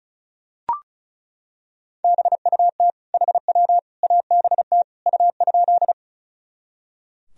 Morse code, heard as a keyed tone, about 36 wpm, 700 Hz. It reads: BUT HW ABT U?
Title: Sentences from Top 100 Words and Abbreviations 35wpm